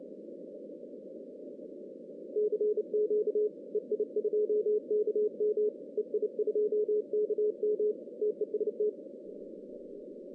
Morse code, heard as a keyed tone, about 29 wpm, 420 Hz. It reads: CQ EI2KM EI2KM TEST
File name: 2015 cq ww cw / 2015 cqwww k5tr - 0-41